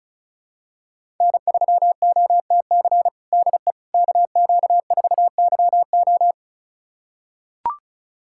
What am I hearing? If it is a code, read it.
Morse code, heard as a keyed tone, about 36 wpm, 700 Hz. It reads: N3OTC DE KQ4YO
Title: USA Callsign DE Callsign - 5 Characters - 35wpm